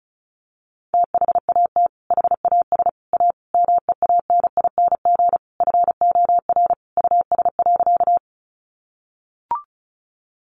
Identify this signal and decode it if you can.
Morse code, heard as a keyed tone, about 34 wpm, 700 Hz. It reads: THAT HAS A MEANING FOR US.
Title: Sentences from Top 900 Words 35wpm